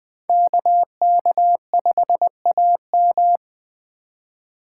Morse code, heard as a keyed tone, about 20 wpm, 700 Hz. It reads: KK5AM